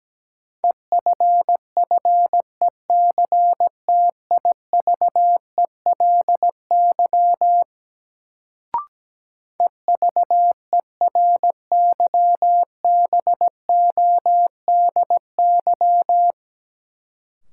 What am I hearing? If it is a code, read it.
Morse code, heard as a keyed tone, about 17 wpm, 700 Hz. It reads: EFFECTIVELY EVERYBODY
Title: Sentences from Top 1400 Words - New Word Drill Alt - 17wpm